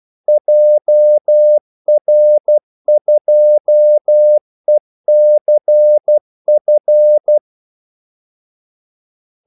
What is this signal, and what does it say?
Morse code, heard as a keyed tone, about 12 wpm, 600 Hz. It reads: JR2ECF
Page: 2 ekstra ordmellemrum mellem kaldesignalerne